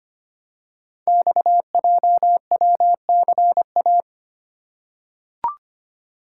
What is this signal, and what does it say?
Morse code, heard as a keyed tone, about 25 wpm, 700 Hz. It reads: XJWCA
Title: Code Groups - 5 Letter:Numbers - 25wpm